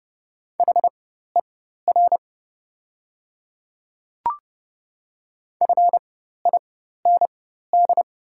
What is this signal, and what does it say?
Morse code, heard as a keyed tone, about 30 wpm, spaced as for 14 wpm, 700 Hz. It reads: HER FIND
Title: 4x Character Spacing - Top 100 Words - 30wpm